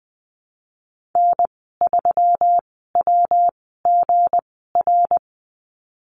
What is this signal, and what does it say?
Morse code, heard as a keyed tone, about 20 wpm, spaced as for 15 wpm, 700 Hz. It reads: N3WGR
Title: 161 Lesson 26g - Letter g - Call Signs 20wpm